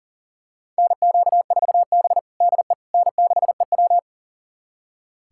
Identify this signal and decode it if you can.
Morse code, heard as a keyed tone, about 40 wpm, 700 Hz. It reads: NQ4B DE N6EW
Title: USA Callsign DE Callsign - 4 Characters - 40wpm